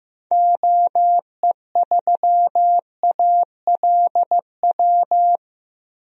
Morse code, heard as a keyed tone, about 15 wpm, 700 Hz. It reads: OE3ALW
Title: DXCC-100 Call Signs - 6 Characters - 15wpm